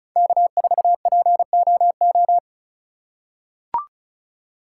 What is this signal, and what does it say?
Morse code, heard as a keyed tone, about 34 wpm, 700 Hz. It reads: K4POO
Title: USA Call Signs - 5 Characters - 4x Character Spacing - 35wpm